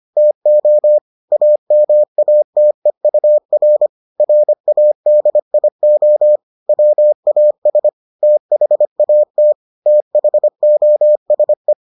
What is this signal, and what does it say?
Morse code, heard as a keyed tone, about 25 wpm, 600 Hz. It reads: TO AMATEUR RADIO WAS THAT THOSE